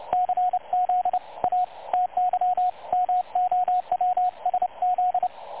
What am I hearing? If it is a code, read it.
Morse code, heard as a keyed tone, about 28 wpm, 730 Hz, with radio noise. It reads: CZATY MOWSZ
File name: dlazaawansowanych / nagraniazpasma / websdr recording 3565.0kHz